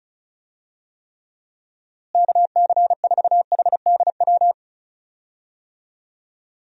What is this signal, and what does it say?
Morse code, heard as a keyed tone, about 36 wpm, 700 Hz. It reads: KC4HDW
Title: USA Call Signs - 6 Characters - Repeated 2x - 35wpm